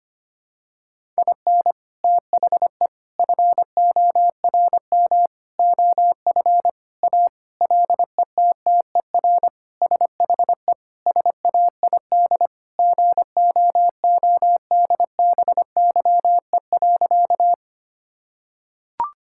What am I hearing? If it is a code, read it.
Morse code, heard as a keyed tone, about 25 wpm, 700 Hz. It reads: IN THE FORM OF A LETTER SHE SAID GOODBYE.
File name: practice / 25wpm / Sentences with 4-Word Phrases - 25wpm